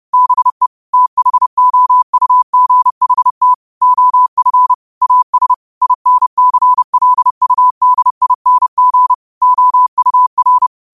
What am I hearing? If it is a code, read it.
Morse code, heard as a keyed tone, about 30 wpm, 1000 Hz. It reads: BE THOUGHT OF AS INCLUDING OUR